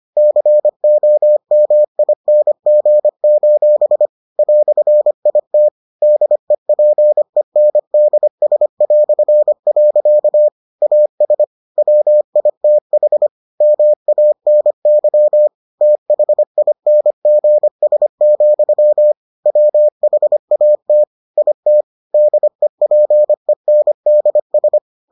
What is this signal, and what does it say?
Morse code, heard as a keyed tone, about 25 wpm, 600 Hz. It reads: COMING: "IT DEPENDS". AS WITH MANY THINGS, WHAT IT DEPENDS